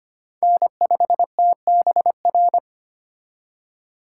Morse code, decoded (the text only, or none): N5TBR